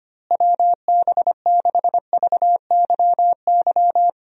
Morse code, heard as a keyed tone, about 25 wpm, 700 Hz. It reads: WB6VYY
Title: USA Call Signs - 6 Characters - Repeated 2x - 25wpm